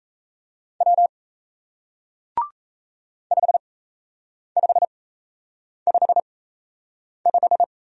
Morse code, heard as a keyed tone, about 37 wpm, spaced as for 8 wpm, 700 Hz. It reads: W 5555